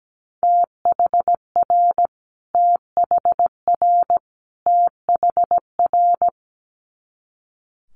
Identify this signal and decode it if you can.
Morse code, heard as a keyed tone, about 17 wpm, 700 Hz. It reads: THR THR THR
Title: Multi-letter Phonemes Repeated 3x 17wpm